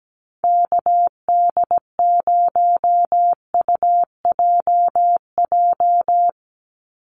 Morse code, heard as a keyed tone, about 17 wpm, 700 Hz. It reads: KD0UJJ
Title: USA Call Signs 17wpm